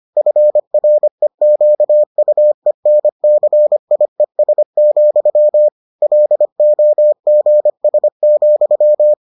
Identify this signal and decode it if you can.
Morse code, heard as a keyed tone, about 25 wpm, 600 Hz. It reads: FREQUENCIES, LOGS,